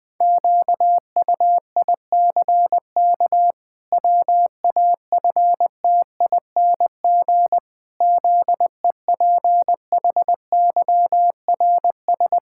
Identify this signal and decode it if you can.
Morse code, heard as a keyed tone, about 20 wpm, 700 Hz. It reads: QUICK WAFTING ZEPHYRS
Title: Pangram Challenge v2 - 20wpm